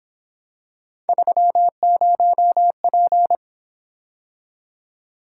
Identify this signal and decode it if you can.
Morse code, heard as a keyed tone, about 26 wpm, 700 Hz. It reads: 30P
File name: Speed-Racing - Three Letter-Number Combinations - 22wpm